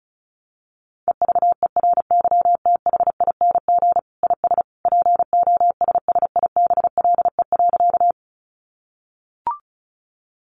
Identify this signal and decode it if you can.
Morse code, heard as a keyed tone, about 34 wpm, 700 Hz. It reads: EVERYTHING IS POSSIBLE.
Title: Sentences from Top 700 Words 35wpm